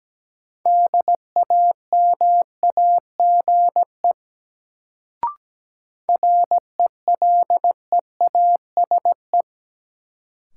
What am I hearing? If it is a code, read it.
Morse code, heard as a keyed tone, about 17 wpm, 700 Hz. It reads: DAMAGE RELEASE
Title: Sentences from Top 1100 Words - New Word Drill - 17wpm